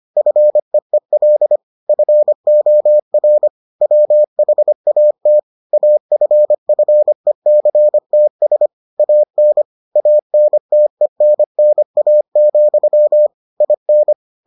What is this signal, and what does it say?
Morse code, heard as a keyed tone, about 25 wpm, 600 Hz. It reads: FEEL FOR WHAT AFFECTS AN ANTENNA, IN